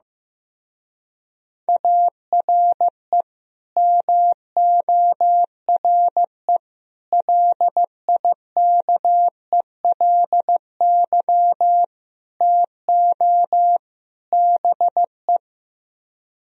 Morse code, heard as a keyed tone, about 15 wpm, 700 Hz. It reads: ARE MORE LIKELY TO BE
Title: Speed-Racing - Sets of 5 Words 15wpm